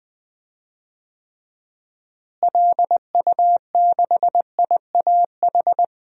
Morse code, heard as a keyed tone, about 21 wpm, 700 Hz. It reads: LU6IAH
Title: DXCC-100 Call Signs - 6 Characters - Repeated 2x - 20wpm